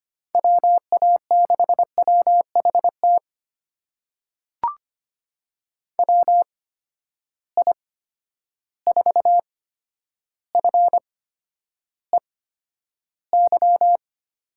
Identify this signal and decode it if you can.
Morse code, heard as a keyed tone, about 25 wpm, spaced as for 11 wpm, 700 Hz. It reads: WA6WHT WI4FEY
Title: USA Call Signs - 6 Characters - 8x Character Spacing - 25wpm